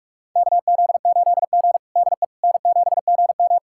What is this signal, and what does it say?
Morse code, heard as a keyed tone, about 44 wpm, 690 Hz. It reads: KZ8G DE N7GM
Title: USA Callsign DE Callsign - 4 Characters - 45wpm